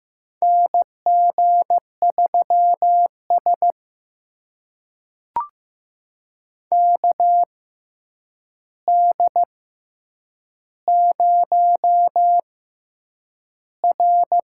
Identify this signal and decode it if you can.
Morse code, heard as a keyed tone, about 15 wpm, spaced as for 8 wpm, 700 Hz. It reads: NG3S KD0R
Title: USA Call Signs - 6x Character Spacing - 15wpm